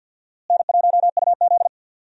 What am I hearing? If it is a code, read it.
Morse code, heard as a keyed tone, about 52 wpm, 700 Hz. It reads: N1UZ
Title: USA Call Signs - 4 Characters - Repeated 2x - 50wpm